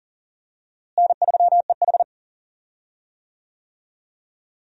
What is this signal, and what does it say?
Morse code, heard as a keyed tone, about 40 wpm, 700 Hz. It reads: N3EH
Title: USA Call Signs - 4 Characters - Repeated 3x - 40wpm